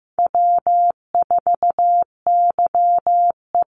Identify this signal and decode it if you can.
Morse code, heard as a keyed tone, about 15 wpm, 700 Hz. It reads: W4YE
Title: Easy Call Signs - 15wpm